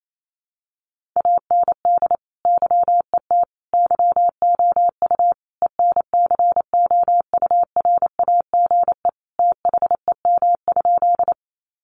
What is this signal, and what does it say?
Morse code, heard as a keyed tone, about 28 wpm, 700 Hz. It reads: AND YET YOU ENCOURAGE THEM?
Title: Sentences from Top 1400 Words 28wpm